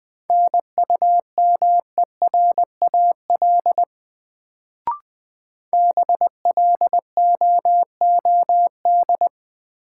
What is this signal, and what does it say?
Morse code, heard as a keyed tone, about 20 wpm, 700 Hz. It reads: NUMERAL BLOOD
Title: Sentences from Top 1000 Words - Review Part 2 - New Word Drill - 20wpm